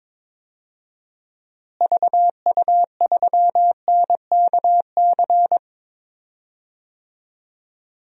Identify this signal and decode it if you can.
Morse code, heard as a keyed tone, about 22 wpm, 700 Hz. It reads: VU3NKC
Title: DXCC-100 Call Signs - 6 Characters - Repeated 3x - 22wpm